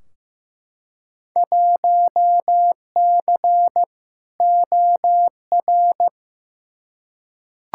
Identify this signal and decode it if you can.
Morse code, heard as a keyed tone, about 15 wpm, 700 Hz. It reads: 1C OR
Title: Field Day Exchanges - 15wpm